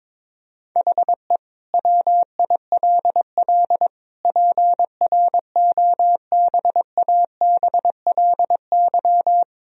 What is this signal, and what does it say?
Morse code, heard as a keyed tone, about 22 wpm, 700 Hz. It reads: HE WILL PROBABLY
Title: Sentences from Top 1000 Words - Review Part 4 - 22wpm